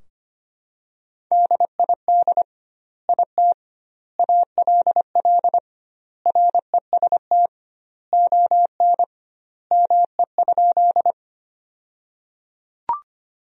Morse code, heard as a keyed tone, about 25 wpm, 700 Hz. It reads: DID IT ALL REST ON ME?
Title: 2x Word-Spacing - Sentences from Top 500 Words - 25wpm